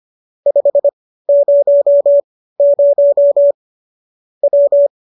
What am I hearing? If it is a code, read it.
Morse code, heard as a keyed tone, about 25 wpm, spaced as for 15 wpm, 570 Hz. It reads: 500 W
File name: qso204 15